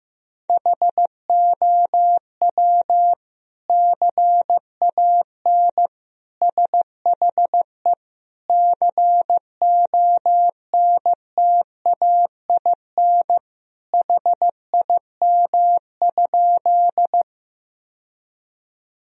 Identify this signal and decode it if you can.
Morse code, heard as a keyed tone, about 16 wpm, 700 Hz. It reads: HOW CAN SHE CONTAIN HIM?
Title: Sentences from Top 1000 Words - Review Part 2 - 15wpm